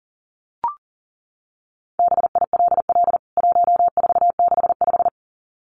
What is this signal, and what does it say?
Morse code, heard as a keyed tone, about 40 wpm, 700 Hz. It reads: BILL 1465
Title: CWT Exchange 40wpm